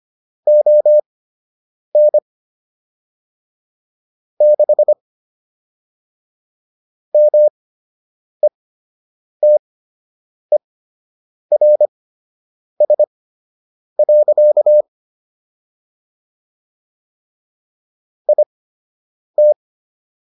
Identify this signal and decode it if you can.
Morse code, heard as a keyed tone, about 25 wpm, spaced as for 8 wpm, 600 Hz. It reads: ON 6 METERS. IT